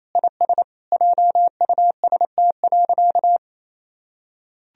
Morse code, heard as a keyed tone, about 28 wpm, 700 Hz. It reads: IS JUST.
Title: Sentences from Top 1000 Words - Review Part 1 - 28wpm